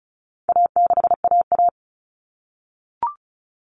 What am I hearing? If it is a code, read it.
Morse code, heard as a keyed tone, about 34 wpm, 700 Hz. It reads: A6AA
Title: All Cut Numbers - 4-Digit 35wpm